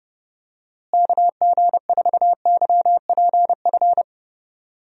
Morse code, heard as a keyed tone, about 30 wpm, 700 Hz. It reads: KG4YPF
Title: USA Call Signs - 6 Characters - Repeated 3x - 30wpm